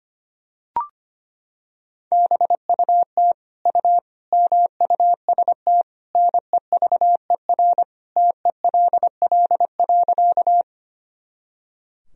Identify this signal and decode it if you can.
Morse code, heard as a keyed tone, about 25 wpm, 700 Hz. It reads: BUT U MUST NEVER TELL.
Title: Sentences from Top 300 Words and Abbreviations 25wpm